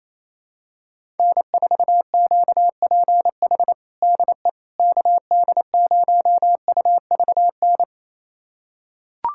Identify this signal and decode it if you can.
Morse code, heard as a keyed tone, about 28 wpm, 700 Hz. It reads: N4QPH DE KD0UVN
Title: USA Callsign DE Callsign 28wpm